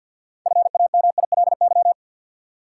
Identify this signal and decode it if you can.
Morse code, heard as a keyed tone, about 48 wpm, 700 Hz. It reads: FAMILY
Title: Six Letter Words - 50wpm